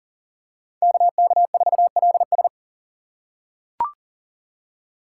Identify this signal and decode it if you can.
Morse code, heard as a keyed tone, about 40 wpm, 700 Hz. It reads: KK4LS
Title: USA Call Signs - 5 Characters - 3x Character Spacing - 40wpm